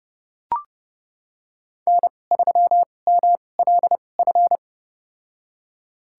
Morse code, heard as a keyed tone, about 29 wpm, spaced as for 21 wpm, 700 Hz. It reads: N3MLF
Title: USA Call Signs - 5 Characters - 2x Character Spacing - 30wpm